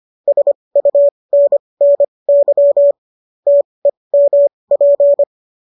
Morse code, heard as a keyed tone, about 25 wpm, spaced as for 20 wpm, 570 Hz. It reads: SUNNY TEMP
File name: qso208 20